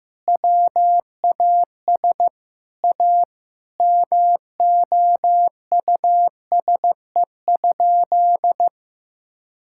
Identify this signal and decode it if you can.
Morse code, heard as a keyed tone, about 15 wpm, 700 Hz. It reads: WAS A MOUSE?
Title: Sentences from Top 1000 Words - Review Part 5 - 15wpm